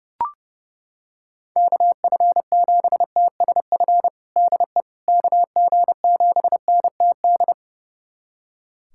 Morse code, heard as a keyed tone, about 30 wpm, 700 Hz. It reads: KF7TSF DE KG7NTD
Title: USA Callsign DE Callsign - 6 Characters - 30wpm